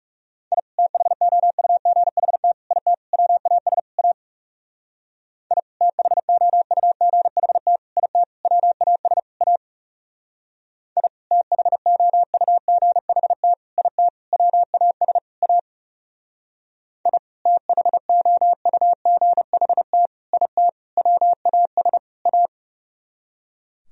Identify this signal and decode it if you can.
Morse code, heard as a keyed tone, about 38 wpm, 700 Hz. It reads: I THOUGHT IT WAS A I THOUGHT IT WAS A I THOUGHT IT WAS A I THOUGHT IT WAS A